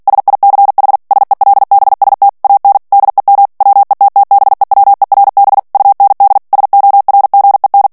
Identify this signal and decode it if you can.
Morse code, heard as a keyed tone, about 48 wpm, 800 Hz. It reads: SICH SELBST AN DEM WETTBEWERB UND SORGEN
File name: morse mp3 / ALLEMAND texte clair 2 vitesse 45 mots minute